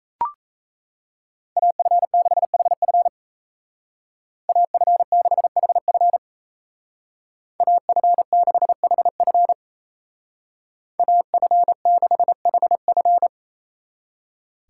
Morse code, heard as a keyed tone, about 36 wpm, 700 Hz. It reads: AF6HF AF6HF AF6HF AF6HF